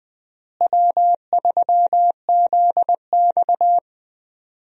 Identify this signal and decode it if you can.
Morse code, heard as a keyed tone, about 20 wpm, 700 Hz. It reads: W3ZX
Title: USA Call Signs - 4 Characters - Repeated 3x - 20wpm